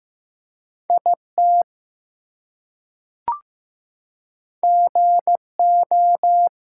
Morse code, heard as a keyed tone, about 15 wpm, 700 Hz. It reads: IT GO
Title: Two Letter Words - 15wpm